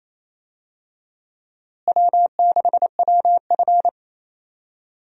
Morse code, heard as a keyed tone, about 28 wpm, 700 Hz. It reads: W6WF
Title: USA Call Signs - 4 Characters - Repeated 2x - 28wpm